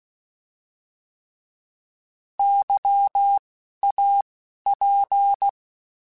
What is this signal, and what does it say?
Morse code, heard as a keyed tone, about 16 wpm, 800 Hz. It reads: Y A P
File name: Alphabetical / Alphabetical mp3s / Ys @10